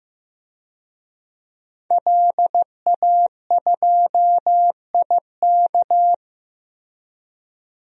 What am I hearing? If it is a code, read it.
Morse code, heard as a keyed tone, about 15 wpm, 700 Hz. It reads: LA2IK
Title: DXCC-100 Call Signs - 5 Characters - Repeated 2x - 15wpm